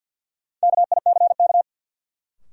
Morse code, heard as a keyed tone, about 52 wpm, 700 Hz. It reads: KICK
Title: Sentences from Top 1300 Words - New Word Drill Alt - 50wpm